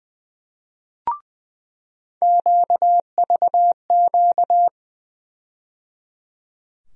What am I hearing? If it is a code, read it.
Morse code, heard as a keyed tone, about 20 wpm, 700 Hz. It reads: QVQ
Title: USA Call Sign Postfixes - 3 Letters 20wpm